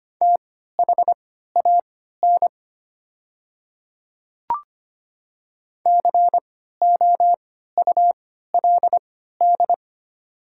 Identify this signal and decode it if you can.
Morse code, heard as a keyed tone, about 25 wpm, spaced as for 14 wpm, 700 Hz. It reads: THAN COULD
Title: 3x Character Spacing - Top 100 Words - 25wpm